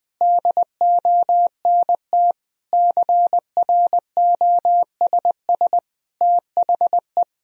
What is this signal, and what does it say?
Morse code, heard as a keyed tone, about 21 wpm, 700 Hz. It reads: DONT CROSS THE
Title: Sentences from Top 1000 Words - Review Part 3 - 20wpm